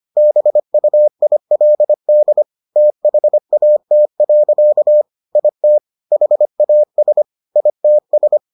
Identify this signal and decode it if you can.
Morse code, heard as a keyed tone, about 25 wpm, 600 Hz. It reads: BUILD THAT. IT HAS ITS